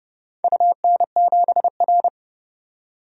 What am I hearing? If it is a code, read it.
Morse code, heard as a keyed tone, about 31 wpm, 700 Hz. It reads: UN7R
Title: DXCC-100 Call Signs - 4 Characters - 30wpm